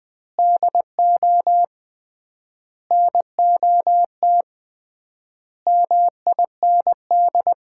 Morse code, heard as a keyed tone, about 21 wpm, 700 Hz. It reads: DO NOT MIND
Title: 3x Word-Spacing - Sets of 5 Words - 20wpm